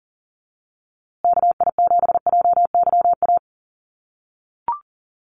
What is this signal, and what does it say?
Morse code, heard as a keyed tone, about 38 wpm, 700 Hz. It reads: KI7JYA
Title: USA Call Signs Repeated 3x 40wpm